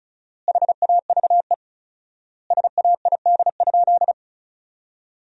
Morse code, heard as a keyed tone, about 36 wpm, 700 Hz. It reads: HAVE SAID?
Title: 4x Word-Spacing - Sentences from Top 100 Words - 35wpm